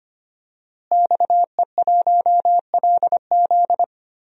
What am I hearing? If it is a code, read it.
Morse code, heard as a keyed tone, about 25 wpm, 700 Hz. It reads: XE1LZ
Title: DXCC-100 Call Signs - 5 Characters - Repeated 2x - 25wpm